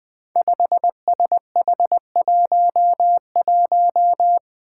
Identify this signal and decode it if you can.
Morse code, heard as a keyed tone, about 20 wpm, 700 Hz. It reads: 5SH11
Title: Five Challenging Letter-Number Combinations 20wpm